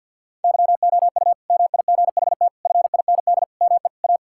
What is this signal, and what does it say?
Morse code, heard as a keyed tone, about 50 wpm, 700 Hz. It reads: YOU MIGHT FIND ME A
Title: Sentences from Top 1600 Words 50wpm